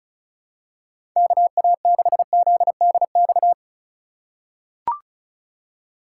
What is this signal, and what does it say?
Morse code, heard as a keyed tone, about 34 wpm, 700 Hz. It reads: KA6ZDX
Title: USA Call Signs - 6 Characters - 2x Character Spacing - 35wpm